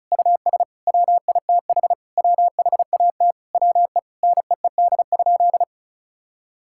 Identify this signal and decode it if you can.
Morse code, heard as a keyed tone, about 36 wpm, 700 Hz. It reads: US WITH WHAT WE NEED?
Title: Sentences from Top 1000 Words - Review Part 5 - 35wpm